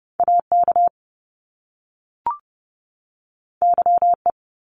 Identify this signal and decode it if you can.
Morse code, heard as a keyed tone, about 30 wpm, 700 Hz. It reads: AK YE
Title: Top Two Letter Combinations in Words 30wpm